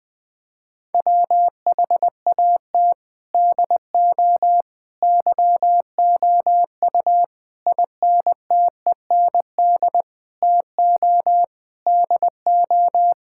Morse code, heard as a keyed tone, about 21 wpm, 700 Hz. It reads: WHAT DO YOU INTEND TO DO
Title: Sentences from Top 1600 Words 20wpm